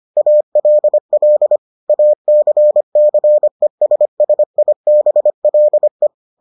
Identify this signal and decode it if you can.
Morse code, heard as a keyed tone, about 25 wpm, 600 Hz. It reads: ALL ACCESSIBLE